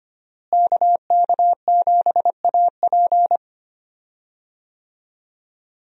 Morse code, heard as a keyed tone, about 25 wpm, 700 Hz. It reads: KK7AP
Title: USA Call Signs - 5 Characters - Repeated 3x - 25wpm